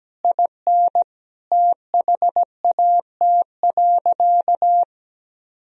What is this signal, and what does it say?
Morse code, heard as a keyed tone, about 17 wpm, 700 Hz. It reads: IN THAT.
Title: Sentences from Top 200 Words and Abbreviations 17wpm